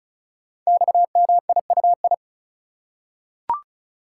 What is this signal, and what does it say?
Morse code, heard as a keyed tone, about 36 wpm, 690 Hz. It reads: XMIUI
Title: Code Groups - 5 Letters - 35wpm